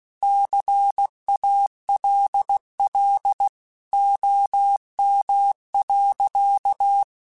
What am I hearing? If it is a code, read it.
Morse code, heard as a keyed tone, about 16 wpm, 800 Hz. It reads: CALL OM.